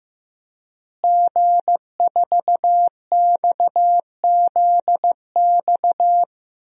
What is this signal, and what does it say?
Morse code, heard as a keyed tone, about 15 wpm, 700 Hz. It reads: G4XZX
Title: DXCC-100 Call Signs - 5 Characters - Repeated 3x - 15wpm